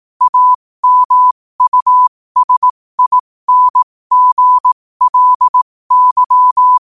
Morse code, heard as a keyed tone, about 18 wpm, spaced as for 15 wpm, 1000 Hz. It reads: AMUSINGLY
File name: qotdfiles / 15wpm